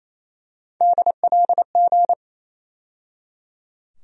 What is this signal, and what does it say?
Morse code, heard as a keyed tone, about 27 wpm, 700 Hz. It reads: DLG
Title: USA Call Sign Postfixes - 3 Letters 28wpm